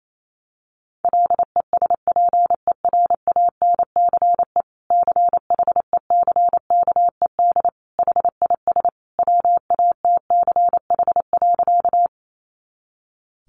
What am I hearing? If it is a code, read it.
Morse code, heard as a keyed tone, about 28 wpm, 700 Hz. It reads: LESPERANCE CHECKED HIS WATCH.
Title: A Sound Of Thunder 28wpm